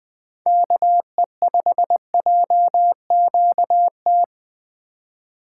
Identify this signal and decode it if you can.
Morse code, heard as a keyed tone, about 20 wpm, 700 Hz. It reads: KE5JQT
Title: USA Call Signs - 4x Character Spacing - 20wpm